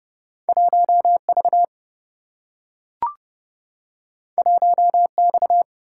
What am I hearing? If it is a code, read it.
Morse code, heard as a keyed tone, about 30 wpm, 700 Hz. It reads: 1 V 1 X
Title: Single-Letter Single-Number 30wpm